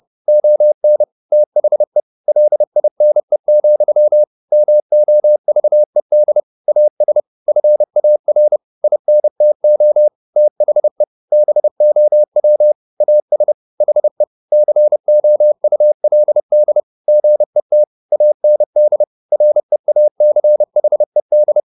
Morse code, heard as a keyed tone, about 30 wpm, 600 Hz. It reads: ON THE LINE, MOVED AS FAR INTO THE BOW AS HE COULD GET AND REACHED
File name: the old man and the sea / the old man and the sea 1 30wpm